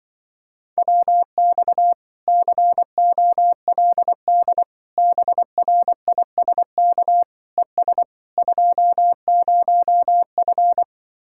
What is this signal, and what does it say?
Morse code, heard as a keyed tone, about 24 wpm, 700 Hz. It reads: WX COLD BRISK ES 20F
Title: Speed-Racing Common QSO Phrases Encore - 20wpm